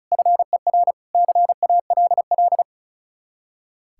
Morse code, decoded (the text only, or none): FER CALL